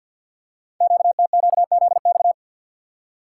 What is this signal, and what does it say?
Morse code, heard as a keyed tone, about 50 wpm, 690 Hz. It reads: QTQZX
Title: Code Groups - 5 Letters - 50wpm